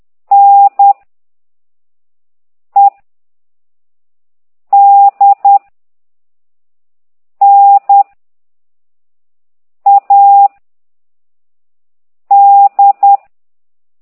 Morse code, heard as a keyed tone, about 10 wpm, 800 Hz. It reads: N E D N A D
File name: ct2idw / exerc15